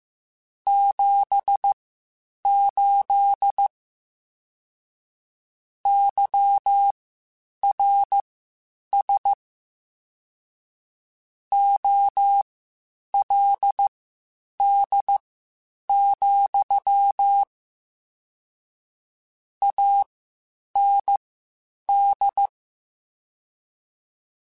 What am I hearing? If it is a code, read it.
Morse code, heard as a keyed tone, about 15 wpm, spaced as for 8 wpm, 800 Hz. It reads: 78 YRS OLD, AND
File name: (spaced) / QSOs Spaced-mp3s / WA7QOX 2 @13 sp